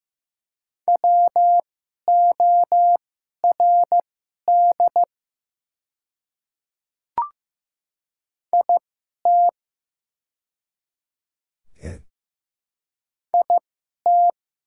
Morse code, heard as a keyed tone, about 15 wpm, spaced as for 11 wpm, 700 Hz. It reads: WORD IT IT
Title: 2x Character Spacing - Top 100 Words - 15wpm